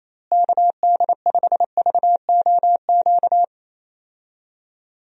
Morse code, heard as a keyed tone, about 28 wpm, 700 Hz. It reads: KD5VOQ
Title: USA Call Signs - 6 Characters - Repeated 2x - 28wpm